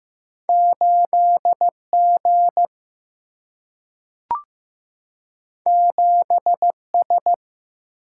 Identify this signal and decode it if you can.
Morse code, heard as a keyed tone, about 15 wpm, 700 Hz. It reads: 8 G 7 S
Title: Single-Letter Single-Number 15wpm